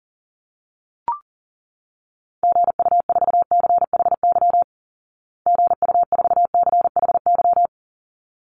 Morse code, heard as a keyed tone, about 40 wpm, 700 Hz. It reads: GU4CHY GU4CHY
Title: Call Signs Repeated 3x - 40wpm